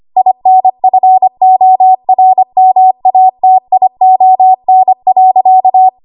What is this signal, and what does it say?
Morse code, heard as a keyed tone, about 25 wpm, 750 Hz. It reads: INFORMATION.
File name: Morse / 25WPM